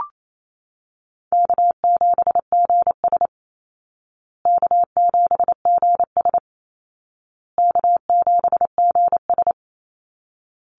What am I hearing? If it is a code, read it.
Morse code, heard as a keyed tone, about 28 wpm, 700 Hz. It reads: K7GS K7GS K7GS
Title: Easy Call Signs Repeated 3x - 28wpm